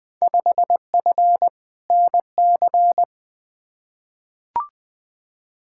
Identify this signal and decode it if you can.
Morse code, heard as a keyed tone, about 20 wpm, 700 Hz. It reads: 5F NC
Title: Field Day Exchanges Encore - 20wpm